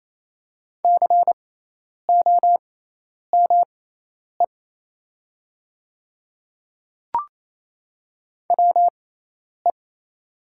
Morse code, heard as a keyed tone, about 28 wpm, spaced as for 10 wpm, 690 Hz. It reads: COME WE
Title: 6x Character Spacing - Top 300 Words - 28wpm